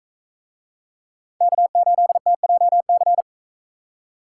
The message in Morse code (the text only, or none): K8TJC